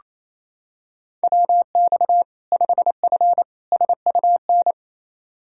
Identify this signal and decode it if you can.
Morse code, heard as a keyed tone, about 28 wpm, 700 Hz. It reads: WX 5F SUN
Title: Practice QSO - Exchange 2 - Info - 28wpm